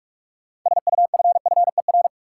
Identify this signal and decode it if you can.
Morse code, heard as a keyed tone, about 46 wpm, 700 Hz. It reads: SUFFER
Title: Six Letter Words - 45wpm